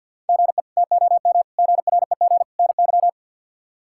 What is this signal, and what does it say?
Morse code, heard as a keyed tone, about 52 wpm, 700 Hz. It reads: GE TOM GREG NY
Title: K1USN SST QSO Practice - 50wpm